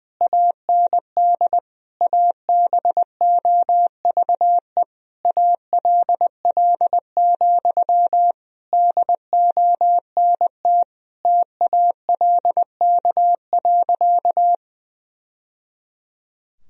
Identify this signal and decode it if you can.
Morse code, heard as a keyed tone, about 20 wpm, 700 Hz. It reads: AND ABOVE ALL, DONT TALK.
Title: Sentences from Top 1000 Words - Review Part 2 - 20wpm